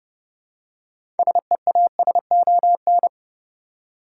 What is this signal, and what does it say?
Morse code, heard as a keyed tone, about 32 wpm, 700 Hz. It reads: SEASON
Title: Six Letter Words - 30wpm